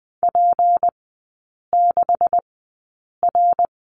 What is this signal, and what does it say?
Morse code, heard as a keyed tone, about 20 wpm, spaced as for 8 wpm, 700 Hz. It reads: P6R
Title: 166 Lesson 27e - Number 7 - Three Characters 20wpm